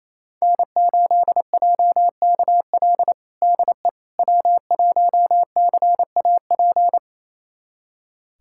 Morse code, heard as a keyed tone, about 28 wpm, 700 Hz. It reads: N8JKL DE W1CAP
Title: USA Callsign DE Callsign - 5 Characters - 28wpm